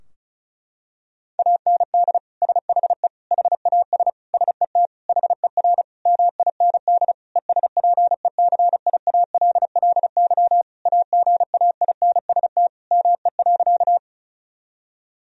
Morse code, heard as a keyed tone, about 36 wpm, 700 Hz. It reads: AND SHE HAS SET HER MIND ESPECIALLY AGAINST ME.
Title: Sentences from Top 1000 Words - Review Part 4 - 35wpm